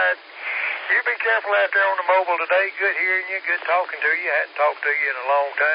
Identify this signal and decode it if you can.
Emergency Alert System (SAME) tones, heard as harsh buzzing bursts, with speech. It is heard from 1.0 to 5.1 s.